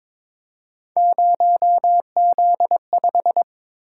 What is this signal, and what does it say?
Morse code, heard as a keyed tone, about 22 wpm, 700 Hz. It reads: 0Z5
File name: practice / 22wpm / Three Letter-Number Combinations - 22wpm